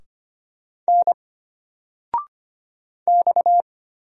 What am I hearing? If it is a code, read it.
Morse code, heard as a keyed tone, about 25 wpm, 700 Hz. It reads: N X
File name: Single-Letter-Number-25wpm